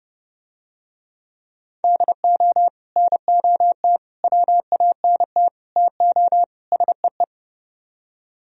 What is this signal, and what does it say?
Morse code, heard as a keyed tone, about 27 wpm, 700 Hz. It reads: DO NOT WANT TO SEE